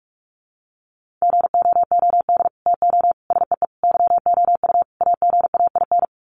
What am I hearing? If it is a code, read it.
Morse code, heard as a keyed tone, about 44 wpm, 700 Hz. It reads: GOOD TO SEE YOU AGAIN
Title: Sets of 5 Words Encore2 45wpm